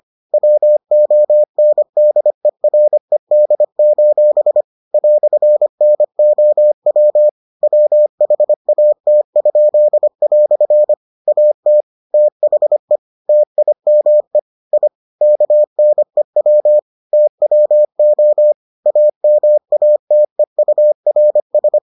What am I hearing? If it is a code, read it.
Morse code, heard as a keyed tone, about 25 wpm, 600 Hz. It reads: WONDERED: "NOW WHAT?" AT THE TIME I KNEW TWO AMATEURS